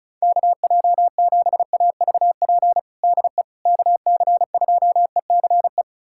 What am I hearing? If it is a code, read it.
Morse code, heard as a keyed tone, about 36 wpm, 700 Hz. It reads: KJ7AVP DE KC2ECE
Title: USA Callsign DE Callsign 35wpm